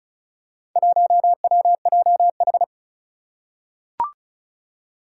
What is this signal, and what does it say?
Morse code, heard as a keyed tone, about 36 wpm, 700 Hz. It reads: 1WJH
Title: Four Challenging Letter-Number Combinations 35wpm